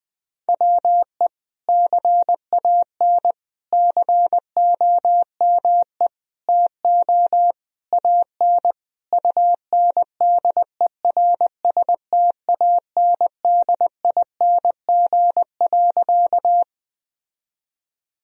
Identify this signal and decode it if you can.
Morse code, heard as a keyed tone, about 20 wpm, 700 Hz. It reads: WE CAN COME TO AN UNDERSTANDING.
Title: Sentences from Top 1000 Words - Review Part 3 - 20wpm